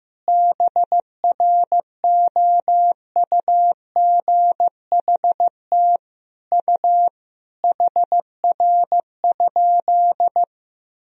Morse code, heard as a keyed tone, about 16 wpm, 700 Hz. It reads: BROUGHT U HR?
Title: Sentences from Top 500 Words and Abbreviations 15wpm